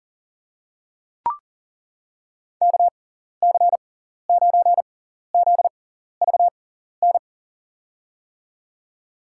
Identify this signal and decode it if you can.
Morse code, heard as a keyed tone, about 37 wpm, spaced as for 14 wpm, 700 Hz. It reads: KC9ZVN
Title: USA Call Signs - 6 Characters - 6x Character Spacing - 40wpm